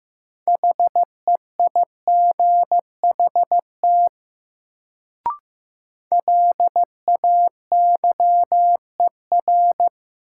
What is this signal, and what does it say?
Morse code, heard as a keyed tone, about 15 wpm, 700 Hz. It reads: HEIGHT LAYER
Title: Sentences from Top 1600 Words - New Word Drill - 15wpm